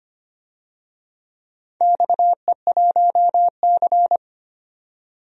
Morse code, heard as a keyed tone, about 25 wpm, 700 Hz. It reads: XE1C
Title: DXCC-100 Call Signs - 4 Characters - 25wpm